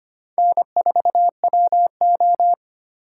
Morse code, heard as a keyed tone, about 34 wpm, 700 Hz. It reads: N4WO